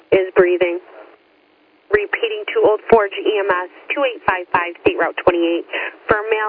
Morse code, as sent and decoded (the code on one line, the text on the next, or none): none